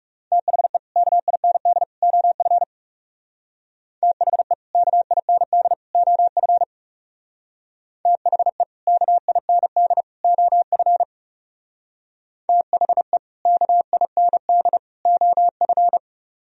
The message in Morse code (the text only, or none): THE KIND OF THE KIND OF THE KIND OF THE KIND OF